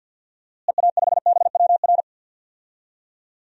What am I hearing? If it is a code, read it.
Morse code, heard as a keyed tone, about 49 wpm, 700 Hz. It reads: EA5BWR
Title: DXCC-100 Call Signs - 6 Characters - Repeated 3x - 50wpm